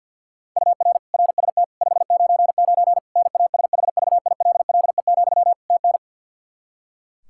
Morse code, heard as a keyed tone, about 48 wpm, 700 Hz. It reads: UR RST 599 NASHVILLE, TN
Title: Practice QSO - Exchange 1 - Beginning - 50wpm